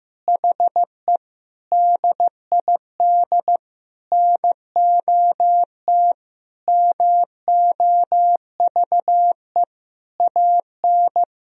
Morse code, heard as a keyed tone, about 15 wpm, 700 Hz. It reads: HE DID NOT MOVE AN
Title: Sentences from Top 1000 Words - Review Part 3 - 15wpm